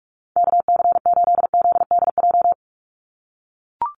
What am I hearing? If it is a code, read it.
Morse code, heard as a keyed tone, about 44 wpm, 690 Hz. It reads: KC8ZDJ
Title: USA Call Signs 45wpm